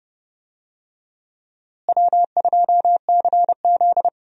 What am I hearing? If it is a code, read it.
Morse code, heard as a keyed tone, about 30 wpm, 700 Hz. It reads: W2CZ
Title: USA Call Signs - 4 Characters - Repeated 2x - 30wpm